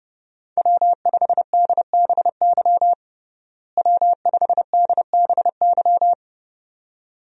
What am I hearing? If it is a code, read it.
Morse code, heard as a keyed tone, about 30 wpm, 700 Hz. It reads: W5DBY W5DBY
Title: USA Call Signs - 5 Characters - Repeated 2x - 30wpm